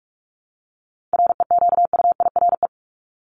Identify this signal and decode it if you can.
Morse code, heard as a keyed tone, about 44 wpm, 700 Hz. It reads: REQUIRE
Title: ED Suffix - Top 100 Words with Base Words 45wpm